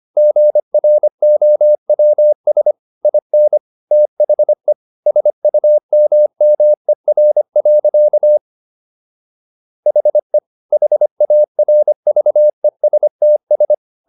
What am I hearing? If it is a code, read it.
Morse code, heard as a keyed tone, about 25 wpm, 600 Hz. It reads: GROWS IN THE SUMMER. HE HARVESTS
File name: SS111 25